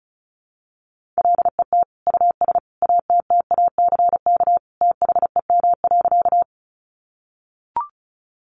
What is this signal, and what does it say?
Morse code, heard as a keyed tone, about 34 wpm, 700 Hz. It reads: LET US ATTACK THEM.
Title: Sentences from Top 1500 Words 35wpm